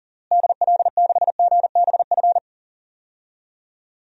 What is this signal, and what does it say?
Morse code, heard as a keyed tone, about 40 wpm, 700 Hz. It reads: DL6GBF
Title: DXCC-100 Call Signs - 6 Characters - Repeated 2x - 40wpm